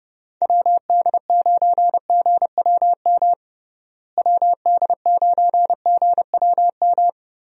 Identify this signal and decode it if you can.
Morse code, heard as a keyed tone, about 30 wpm, 700 Hz. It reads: WD9GWM WD9GWM
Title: USA Call Signs - 6 Characters - Repeated 2x - 30wpm